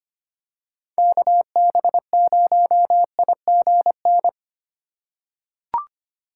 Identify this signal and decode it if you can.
Morse code, heard as a keyed tone, about 25 wpm, 700 Hz. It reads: KB0IGN
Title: USA Call Signs - 6 Characters - 4x Character Spacing - 25wpm